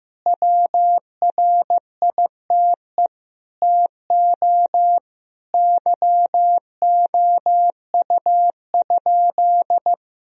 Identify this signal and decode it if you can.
Morse code, heard as a keyed tone, about 15 wpm, 700 Hz. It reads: WRITE TO YOU?
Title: Sentences from Top 1000 Words - Review Part 3 - 15wpm